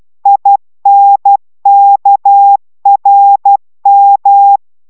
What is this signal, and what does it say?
Morse code, heard as a keyed tone, about 12 wpm, 800 Hz. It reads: INKRM